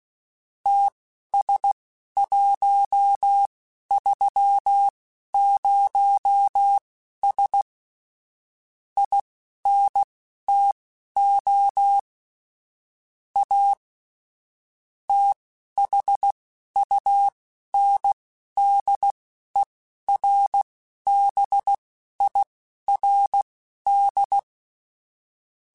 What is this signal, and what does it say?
Morse code, heard as a keyed tone, about 16 wpm, spaced as for 11 wpm, 800 Hz. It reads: TS130S INTO A THUNDERBIRD